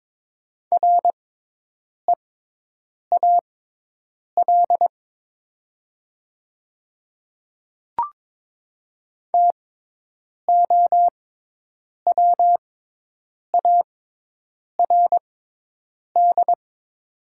Morse code, heard as a keyed tone, about 22 wpm, spaced as for 8 wpm, 700 Hz. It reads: REAL TOWARD
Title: 6x Character Spacing - Top 500 Words - 22wpm